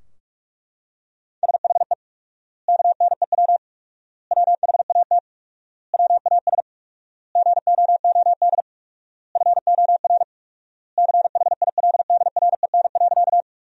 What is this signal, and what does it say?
Morse code, heard as a keyed tone, about 42 wpm, 700 Hz. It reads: SHE KNEW WHAT WAS GOOD FOR CHILDREN.
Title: 4x Word-Spacing - Sentences from Top 400 Words - 45wpm